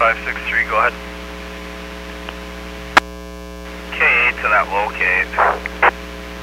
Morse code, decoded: none

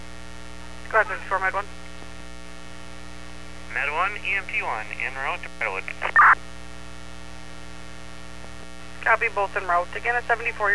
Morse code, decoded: none

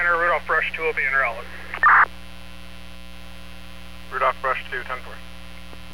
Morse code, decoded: none

